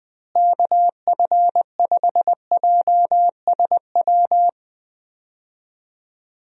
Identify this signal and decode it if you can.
Morse code, heard as a keyed tone, about 20 wpm, 700 Hz. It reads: KF5JSW
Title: USA Call Signs - 6 Characters - Repeated 3x - 20wpm